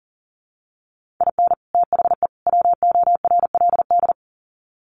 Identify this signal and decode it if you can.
Morse code, heard as a keyed tone, about 40 wpm, 700 Hz. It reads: IN THE WORLD
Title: Sets of 3 Words 40wpm